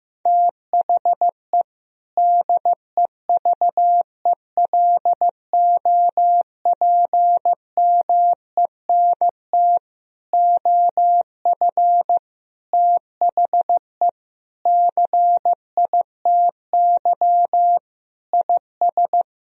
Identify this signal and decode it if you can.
Morse code, heard as a keyed tone, about 15 wpm, 700 Hz. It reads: THE DEVELOPMENT OF THE CITY IS
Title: Sentences from Top 1000 Words - Review Part 4 - 15wpm